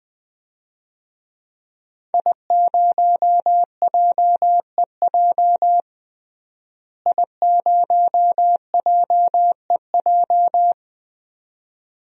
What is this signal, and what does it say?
Morse code, heard as a keyed tone, about 20 wpm, 700 Hz. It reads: I0JEJ I0JEJ
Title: DXCC-100 Call Signs - 5 Characters - Repeated 2x - 20wpm